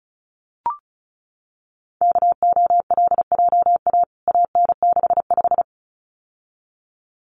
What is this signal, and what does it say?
Morse code, heard as a keyed tone, about 34 wpm, 700 Hz. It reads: KOLJA AN65
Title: CWT Exchange 35wpm